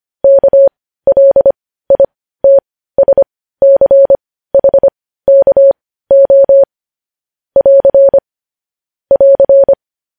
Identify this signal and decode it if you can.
Morse code, heard as a keyed tone, about 25 wpm, spaced as for 15 wpm, 550 Hz. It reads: KLITSCHKO <AR> <AR>
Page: news-headlines-in-morse-code-at-15-wpm